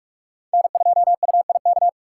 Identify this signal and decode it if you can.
Morse code, heard as a keyed tone, about 38 wpm, 700 Hz. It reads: N2UIK